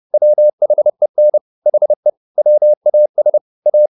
Morse code, heard as a keyed tone, about 30 wpm, 600 Hz. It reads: WHEN HE WAS A
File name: the old man and the sea / the old man and the sea 1 30wpm